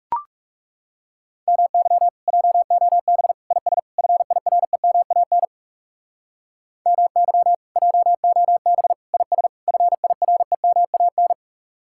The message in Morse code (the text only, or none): MY JOB IS FIREMAN MY JOB IS FIREMAN